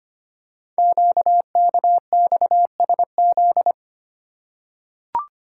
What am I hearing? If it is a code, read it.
Morse code, heard as a keyed tone, about 25 wpm, 700 Hz. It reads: QKXSZ
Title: Code Groups - 5 Letters - 25wpm